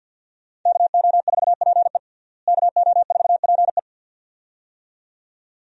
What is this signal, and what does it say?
Morse code, heard as a keyed tone, about 51 wpm, 700 Hz. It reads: KO4PE KO4PE
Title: USA Call Signs - 5 Characters - Repeated 2x - 50wpm